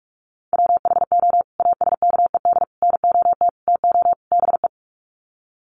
Morse code, heard as a keyed tone, about 44 wpm, 700 Hz. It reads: WHO ASKED NOT TO BE
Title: Sets of 5 Words Encore2 45wpm